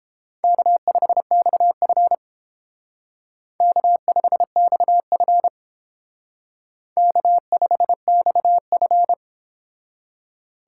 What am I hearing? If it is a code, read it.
Morse code, heard as a keyed tone, about 30 wpm, 700 Hz. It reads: K5XF K5XF K5XF